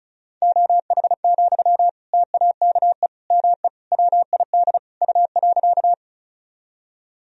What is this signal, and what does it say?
Morse code, heard as a keyed tone, about 36 wpm, 700 Hz. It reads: OH, TAKE ME WID U.
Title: Sentences from Top 500 Words and Abbreviations 35wpm